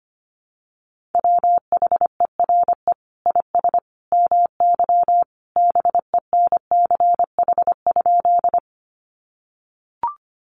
Morse code, heard as a keyed tone, about 25 wpm, 700 Hz. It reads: WHERE IS MY BENCH?
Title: Sentences from Top 1400 Words 25wpm